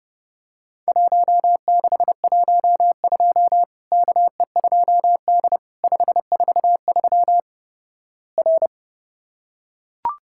Morse code, heard as a keyed tone, about 30 wpm, 700 Hz. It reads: 1612 KE2D 543 R
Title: Worked All Europe DX Contest - QTC Exchange - 30wpm